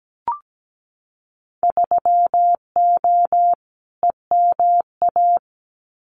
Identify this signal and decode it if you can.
Morse code, heard as a keyed tone, about 17 wpm, 700 Hz. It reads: 3O EMA
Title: Winter Field Day Exchanges 17wpm